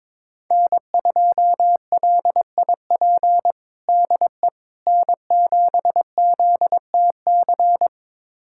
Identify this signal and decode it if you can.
Morse code, heard as a keyed tone, about 22 wpm, 700 Hz. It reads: N2LIP DE N7ZTC
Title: USA Callsign DE Callsign - 5 Characters - 22wpm